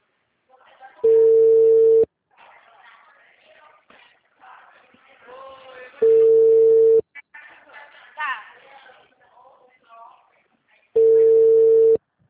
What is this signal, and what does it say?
Ringback tone (China, 450 Hz), with speech.